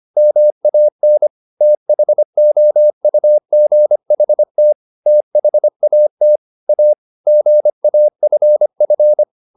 Morse code, heard as a keyed tone, about 25 wpm, 600 Hz. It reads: MAN THOUGHT THAT A GAFF
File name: the old man and the sea part 1 25wpm